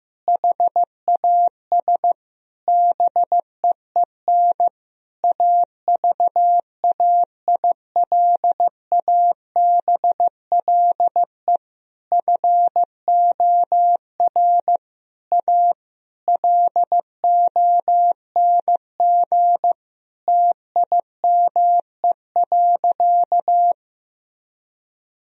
Morse code, heard as a keyed tone, about 15 wpm, 700 Hz. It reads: HAS BEEN AVAILABLE FOR A LONG TIME.
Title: Sentences from Top 1000 Words - Review Part 4 - 15wpm